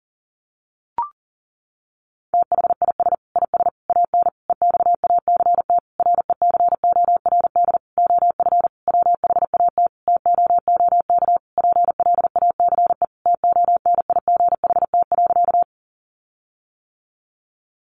Morse code, heard as a keyed tone, about 40 wpm, 700 Hz. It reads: THIS IS AN EXACT RECORD OF WHAT TOOK PLACE TONIGHT.
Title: Sentences from Top 1300 Words 40wpm